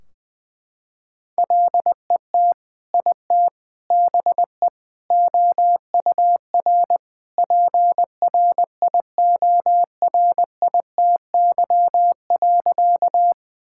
Morse code, heard as a keyed tone, about 20 wpm, 700 Hz. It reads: LET IT BE OUR PRIORITY.
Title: Sentences from Top 1600 Words 20wpm